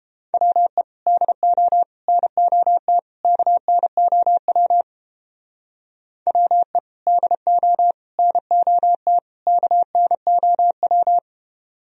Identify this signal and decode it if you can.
Morse code, heard as a keyed tone, about 29 wpm, 700 Hz. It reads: WE DO NOT KNOW WE DO NOT KNOW